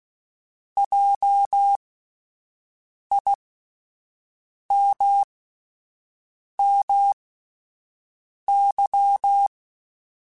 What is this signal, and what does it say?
Morse code, heard as a keyed tone, about 16 wpm, spaced as for 5 wpm, 800 Hz. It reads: JIMMY